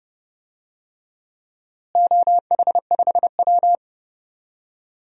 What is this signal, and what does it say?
Morse code, heard as a keyed tone, about 30 wpm, 700 Hz. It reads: OH5W
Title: DXCC-100 Call Signs - 4 Characters - Repeated 3x - 30wpm